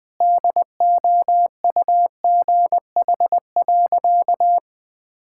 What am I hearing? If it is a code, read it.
Morse code, heard as a keyed tone, about 20 wpm, 700 Hz. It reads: DOUGH.
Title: All-Age Appropriate Jokes - 20wpm